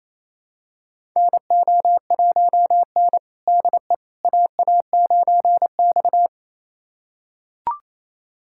Morse code, heard as a keyed tone, about 28 wpm, 700 Hz. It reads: NO1N DE AA9X
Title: USA Callsign DE Callsign - 4 Characters - 28wpm